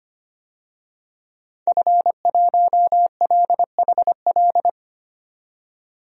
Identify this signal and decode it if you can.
Morse code, heard as a keyed tone, about 25 wpm, 700 Hz. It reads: F1LHL
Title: DXCC-100 Call Signs - 5 Characters - 25wpm